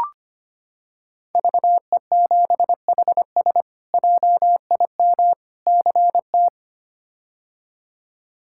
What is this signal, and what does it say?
Morse code, heard as a keyed tone, about 25 wpm, 700 Hz. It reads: VE7HS JIM CT
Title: K1USN SST QSO Practice - 25wpm